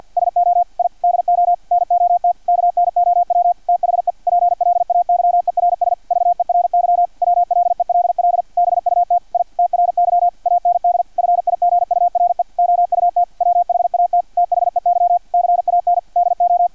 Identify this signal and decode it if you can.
Morse code, heard as a keyed tone, about 50 wpm, 700 Hz, with radio noise. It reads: SO I DO NOT KNOW THE PLAYERS VERY WLELL BUT I TRY AND FIGURE OUT WHAT THEY CAN DO
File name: archived / CRSnet